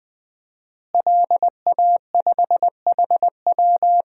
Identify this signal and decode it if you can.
Morse code, heard as a keyed tone, about 20 wpm, 700 Hz. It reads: LA5HW
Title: DXCC-100 Call Signs - 5 Characters - Repeated 3x - 20wpm